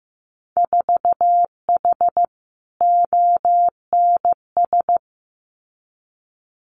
Morse code, heard as a keyed tone, about 15 wpm, 700 Hz. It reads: 4H ONS
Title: Winter Field Day Exchanges Encore 15wpm